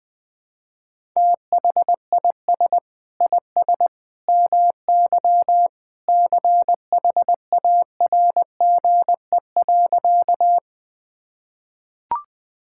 Morse code, heard as a keyed tone, about 21 wpm, 700 Hz. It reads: THIS IS MY CHARGE.
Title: Sentences from Top 1000 Words - Review Part 6 - 20wpm